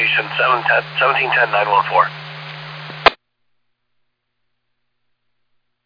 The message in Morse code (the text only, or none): none